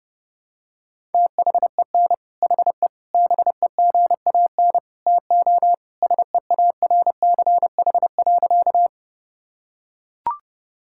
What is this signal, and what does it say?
Morse code, heard as a keyed tone, about 32 wpm, 700 Hz. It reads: THEN HE BEGAN TO SEARCH.
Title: Sentences from Top 1000 Words - Review Part 5 - 30wpm